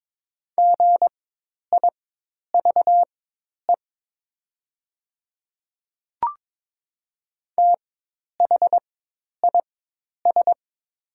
Morse code, heard as a keyed tone, about 22 wpm, spaced as for 10 wpm, 700 Hz. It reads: GIVE THIS
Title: 4x Character Spacing - Top 300 Words - 22wpm